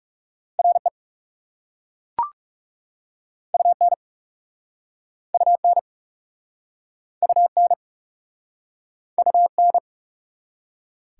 Morse code, heard as a keyed tone, about 40 wpm, 700 Hz. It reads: RE UN UN UN UN